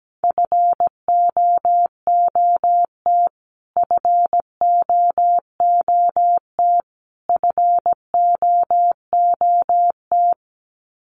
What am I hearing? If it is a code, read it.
Morse code, heard as a keyed tone, about 17 wpm, 700 Hz. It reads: FOOT FOOT FOOT
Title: Top 500 Words Repeated 17wpm